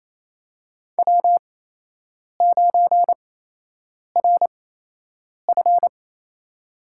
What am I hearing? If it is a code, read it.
Morse code, heard as a keyed tone, about 28 wpm, spaced as for 8 wpm, 700 Hz. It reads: W9RF
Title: USA Call Signs - 4 Characters - 8x Character Spacing - 28wpm